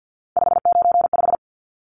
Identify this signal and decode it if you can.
Morse code, heard as a keyed tone, about 50 wpm, 700 Hz. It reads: 595
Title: Numbers - 3-Digit 50wpm